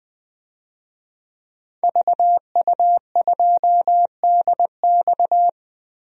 Morse code, heard as a keyed tone, about 21 wpm, 700 Hz. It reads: VU2DX
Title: DXCC-100 Call Signs - 5 Characters - 20wpm